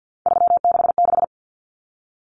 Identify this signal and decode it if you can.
Morse code, heard as a keyed tone, about 50 wpm, 700 Hz. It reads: 366
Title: Numbers - 3-Digit 50wpm